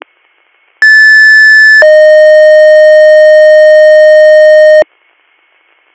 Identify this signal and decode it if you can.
Two-tone page, heard as a one-second tone, then a three-second tone, about 1750 then 620 Hz. It is heard from 0.8 to 4.8 s.